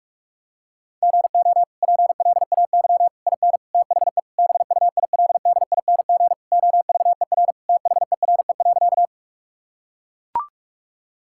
Morse code, heard as a keyed tone, about 42 wpm, 700 Hz. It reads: GO PLAY IN THE BUILDING OVER THERE.
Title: Sentences from Top 1000 Words - Review Part 5 - 45wpm